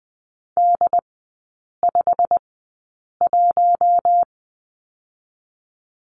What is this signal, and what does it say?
Morse code, heard as a keyed tone, about 20 wpm, spaced as for 8 wpm, 700 Hz. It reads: D51
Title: 087 Lesson 15e - Number 5 - Three Characters 20wpm